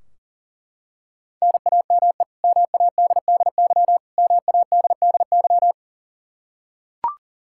Morse code, heard as a keyed tone, about 40 wpm, 700 Hz. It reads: NAME MADDY MADDY
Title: Practice QSO - Exchange 1 - Beginning - 40wpm